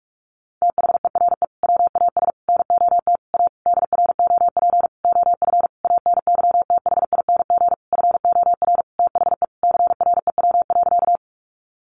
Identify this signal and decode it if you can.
Morse code, heard as a keyed tone, about 44 wpm, 700 Hz. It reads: THERE WAS NOT A DROP OF ANYTHING FOR THE CREW.
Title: Sentences from Top 1500 Words 45wpm